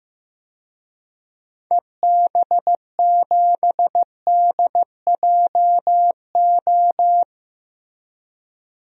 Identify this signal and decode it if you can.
Morse code, heard as a keyed tone, about 15 wpm, 700 Hz. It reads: EB7DJO
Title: DXCC-100 Call Signs - 6 Characters - Repeated 2x - 15wpm